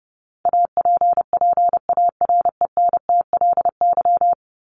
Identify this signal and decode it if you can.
Morse code, heard as a keyed tone, about 29 wpm, 700 Hz. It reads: APPARENTLY
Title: LY Suffix - Top 100 Words with Base Words 30wpm